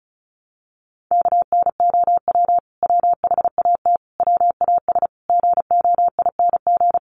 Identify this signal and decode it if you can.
Morse code, heard as a keyed tone, about 34 wpm, 700 Hz. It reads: KNOW WHAT WAS GOING
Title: Sets of 5 Words Encore2 35wpm